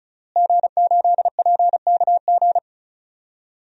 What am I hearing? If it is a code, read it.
Morse code, heard as a keyed tone, about 36 wpm, 700 Hz. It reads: G8PKG
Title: DXCC-100 Call Signs - 5 Characters - Repeated 3x - 35wpm